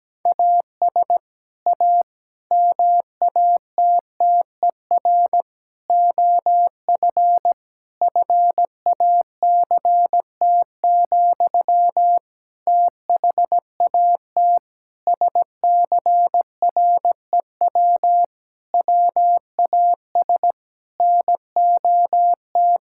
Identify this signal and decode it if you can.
Morse code, heard as a keyed tone, about 17 wpm, 700 Hz. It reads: AS A MATTER OF FACT, THAT SCREW WAS NOT
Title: Sentences from Top 1600 Words 17wpm